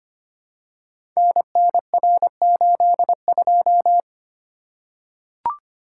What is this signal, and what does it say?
Morse code, heard as a keyed tone, about 25 wpm, 700 Hz. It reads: NNR82
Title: Code Groups - 5 Letter:Numbers - 25wpm